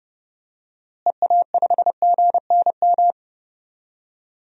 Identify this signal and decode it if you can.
Morse code, heard as a keyed tone, about 30 wpm, 700 Hz. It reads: EA5GNM
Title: DXCC-100 Call Signs - 6 Characters - Repeated 2x - 30wpm